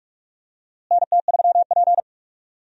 Morse code, heard as a keyed tone, about 44 wpm, 700 Hz. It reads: NT3P
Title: USA Call Signs - 4 Characters - Repeated 3x - 45wpm